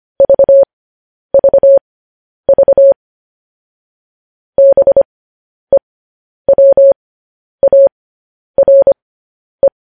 Morse code, heard as a keyed tone, about 25 wpm, spaced as for 10 wpm, 550 Hz. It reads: VVV BEWARE
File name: podcast / News headlines-1x-25x10